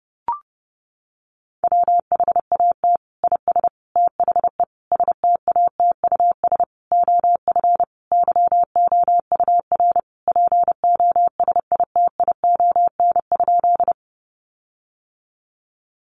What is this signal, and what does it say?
Morse code, heard as a keyed tone, about 30 wpm, 700 Hz. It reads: WHAT IS THE STATUS OF YOUR POSITION?
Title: Sentences from Top 1400 Words 30wpm